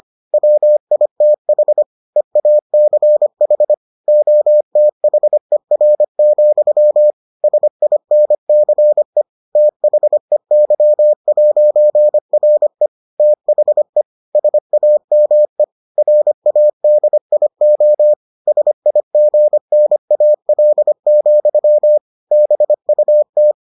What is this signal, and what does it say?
Morse code, heard as a keyed tone, about 25 wpm, 600 Hz. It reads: WITH EACH OTHER, SINCE THEY'RE THE SAME RADIO SIGNAL, BUT